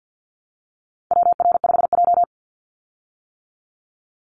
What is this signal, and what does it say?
Morse code, heard as a keyed tone, about 50 wpm, 700 Hz. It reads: WR5J
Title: Call Signs - 50wpm